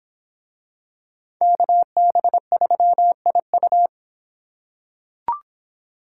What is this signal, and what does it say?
Morse code, heard as a keyed tone, about 25 wpm, 700 Hz. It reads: KB3IU